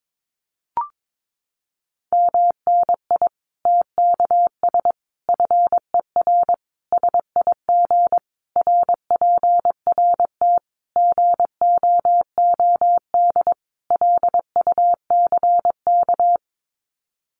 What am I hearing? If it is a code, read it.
Morse code, heard as a keyed tone, about 22 wpm, 700 Hz. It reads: MNI TKS FER SIG RPRT GOOD LUCK
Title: Common QSO Phrases 22wpm